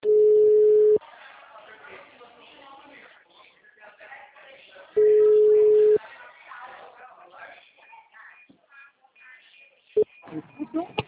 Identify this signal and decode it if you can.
Ringback tone (Europe, 425 Hz), with speech.